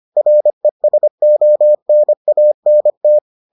Morse code, heard as a keyed tone, about 25 wpm, 600 Hz. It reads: RESONANT